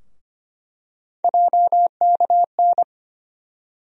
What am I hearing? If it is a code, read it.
Morse code, heard as a keyed tone, about 25 wpm, 700 Hz. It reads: JKN
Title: USA Call Sign Postfixes - 3 Letters 25wpm